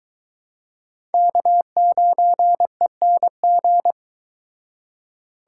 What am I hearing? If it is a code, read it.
Morse code, heard as a keyed tone, about 24 wpm, 700 Hz. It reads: K9ENG